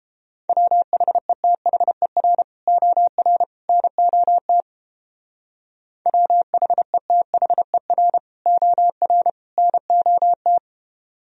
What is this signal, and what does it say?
Morse code, heard as a keyed tone, about 30 wpm, 700 Hz. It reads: WHETHER OR NOT WHETHER OR NOT